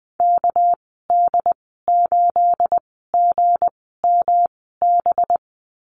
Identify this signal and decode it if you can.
Morse code, heard as a keyed tone, about 20 wpm, spaced as for 15 wpm, 700 Hz. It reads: KD8GMB
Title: 238 Lesson 38g - Letter q - Call Signs 20wpm